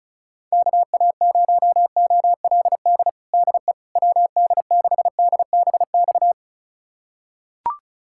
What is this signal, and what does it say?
Morse code, heard as a keyed tone, about 34 wpm, 700 Hz. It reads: KA0OLD DE WD6DBX
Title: USA Callsign DE Callsign - 6 Characters - 35wpm